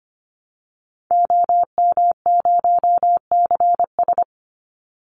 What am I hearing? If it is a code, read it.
Morse code, heard as a keyed tone, about 25 wpm, 700 Hz. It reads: OM0CS
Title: Call Signs Repeated 2x - 25wpm